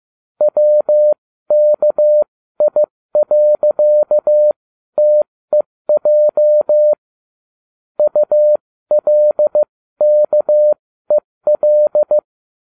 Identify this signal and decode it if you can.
Morse code, heard as a keyed tone, about 15 wpm, spaced as for 13 wpm, 600 Hz. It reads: WKI.TEJ ULKEL